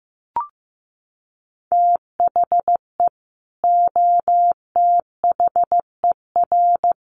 Morse code, heard as a keyed tone, about 15 wpm, 700 Hz. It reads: THE OTHER
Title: Sets of 5 Words 15wpm